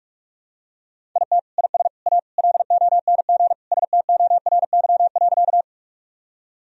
Morse code, heard as a keyed tone, about 41 wpm, 700 Hz. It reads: IT IS A LONG STORY.
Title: Sentences from Top 1000 Words - Review Part 1 - 45wpm